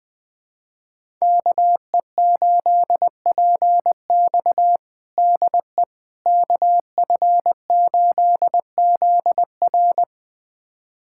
Morse code, heard as a keyed tone, about 21 wpm, 700 Hz. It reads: KE8PX DE KF8ZR
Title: USA Callsign DE Callsign - 5 Characters - 20wpm